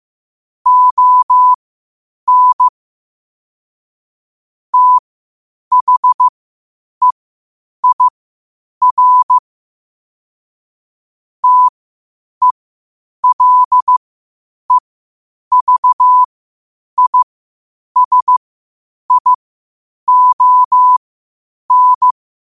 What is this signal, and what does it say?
Morse code, heard as a keyed tone, about 15 wpm, spaced as for 8 wpm, 1000 Hz. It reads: ON THEIR TELEVISION
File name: qotdfiles / 8wpm